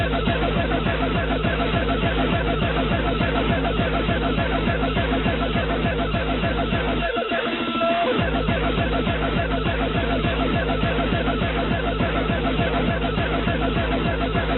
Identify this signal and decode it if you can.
Two-tone teleprinter signal, about 200 Hz shift.